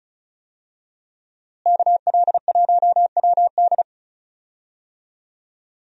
Morse code, heard as a keyed tone, about 34 wpm, 700 Hz. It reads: KL1WD
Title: USA Call Signs - 5 Characters - Repeated 2x - 35wpm